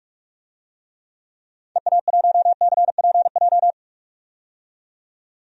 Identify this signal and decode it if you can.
Morse code, heard as a keyed tone, about 44 wpm, 700 Hz. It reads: EA1CPJ
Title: DXCC-100 Call Signs - 6 Characters - 45wpm